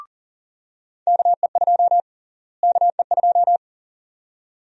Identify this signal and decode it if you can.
Morse code, heard as a keyed tone, about 40 wpm, 700 Hz. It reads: KE2 KE2
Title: USA Call Sign Prefixes - Repeated 2x - 40wpm